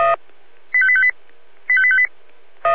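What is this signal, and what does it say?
Five-tone selective call (ZVEI), heard as quick runs of five tones, tones of about 70 ms. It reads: 85858 85858
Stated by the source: Five-tone ZVEI selective call received on SDR, recorded by feuerwehr-orlamuende (fire brigade Orlamünde).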